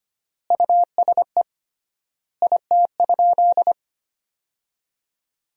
Morse code, heard as a keyed tone, about 25 wpm, 700 Hz. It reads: USE IT?
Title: 3x Word-Spacing - Sentences from Top 200 Words - 25wpm